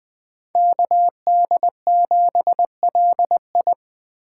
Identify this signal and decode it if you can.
Morse code, heard as a keyed tone, about 20 wpm, 700 Hz. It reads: KD7LI
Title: Speed Racing - USA 2x2 Call Signs 17wpm